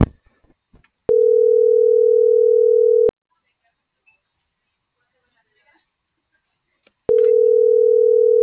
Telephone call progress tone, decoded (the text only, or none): ringback tone